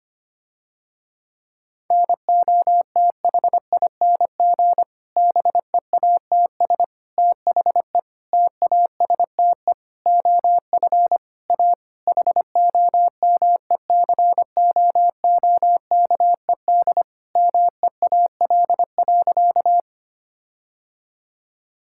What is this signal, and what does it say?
Morse code, heard as a keyed tone, about 25 wpm, 700 Hz. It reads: NOTHING BEATS THE TASTE OF A HOMECOOKED MEAL.
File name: practice / 25wpm / Difficult Sentences with Hints - 25wpm